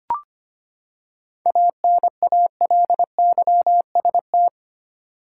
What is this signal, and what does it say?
Morse code, heard as a keyed tone, about 25 wpm, 700 Hz. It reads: ANALYST
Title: Practice QSO - Exchange 3 - More Info - 25wpm